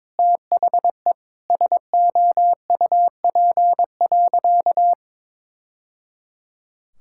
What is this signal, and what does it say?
Morse code, heard as a keyed tone, about 22 wpm, 700 Hz. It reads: THE SOUP.
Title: Sentences from Top 1000 Words - Review Part 6 - 22wpm